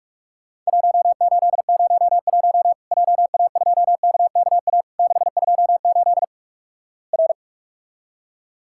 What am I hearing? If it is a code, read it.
Morse code, heard as a keyed tone, about 45 wpm, 700 Hz. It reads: 1801 JA2KKA 628 R
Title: Worked All Europe DX Contest - QTC Exchange - 45wpm